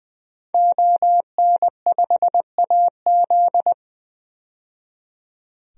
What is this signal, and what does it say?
Morse code, heard as a keyed tone, about 20 wpm, 700 Hz. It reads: ON5AZ
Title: DXCC-100 Call Signs - 5 Characters - 20wpm